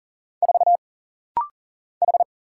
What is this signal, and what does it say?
Morse code, heard as a keyed tone, about 40 wpm, 700 Hz. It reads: 4 H
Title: Single Letter-Number - Rapid-Fire 40wpm